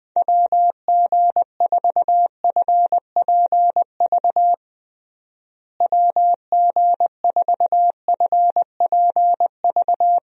Morse code, heard as a keyed tone, about 21 wpm, 700 Hz. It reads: WG4FPV WG4FPV
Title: USA Call Signs - 6 Characters - Repeated 2x - 20wpm